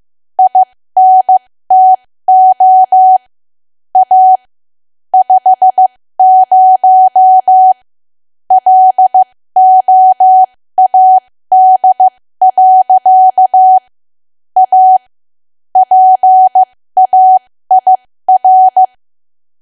Morse code, heard as a keyed tone, about 15 wpm, spaced as for 13 wpm, 750 Hz. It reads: INTO A 50 LOAD. A PAIR
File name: cwa / W1AW-Level2 / 13WPM